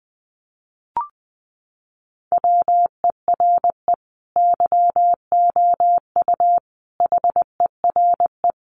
Morse code, heard as a keyed tone, about 20 wpm, 700 Hz. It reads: WERE YOU HERE
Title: Sentences from Top 200 Words 20wpm